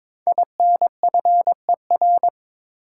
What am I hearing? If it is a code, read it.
Morse code, heard as a keyed tone, about 22 wpm, 700 Hz. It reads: INFER
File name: 5 Letter Words - First 3 Letters Given - 22wpm